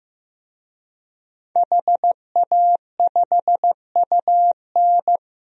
Morse code, heard as a keyed tone, about 15 wpm, 700 Hz. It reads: HA5UN
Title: DXCC-100 Call Signs - 5 Characters - 15wpm